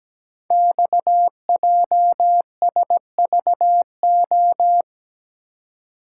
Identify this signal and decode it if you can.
Morse code, heard as a keyed tone, about 17 wpm, 700 Hz. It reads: XJSVO
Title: Code Groups - 5 Letters - 17wpm